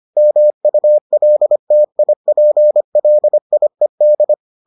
Morse code, heard as a keyed tone, about 25 wpm, 600 Hz. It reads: MULTIPLIED